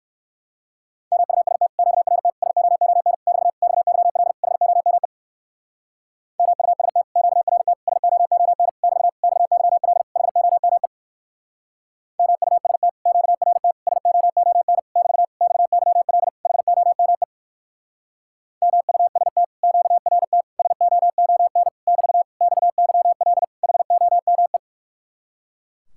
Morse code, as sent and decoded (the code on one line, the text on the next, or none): -- ..- ... - / --.- .-. - / ... --- --- -. / -...- / -..- -.-- .-.. / .... --- -- . / -- ..- ... - / --.- .-. - / ... --- --- -. / -...- / -..- -.-- .-.. / .... --- -- . / -- ..- ... - / --.- .-. - / ... --- --- -. / -...- / -..- -.-- .-.. / .... --- -- . / -- ..- ... - / --.- .-. - / ... --- --- -. / -...- / -..- -.-- .-.. / .... --- -- .
MUST QRT SOON <BT> XYL HOME MUST QRT SOON <BT> XYL HOME MUST QRT SOON <BT> XYL HOME MUST QRT SOON <BT> XYL HOME